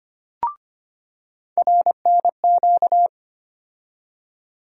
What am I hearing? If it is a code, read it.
Morse code, heard as a keyed tone, about 25 wpm, 700 Hz. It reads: RNQ
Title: USA Call Sign Postfixes - 3 Letters 25wpm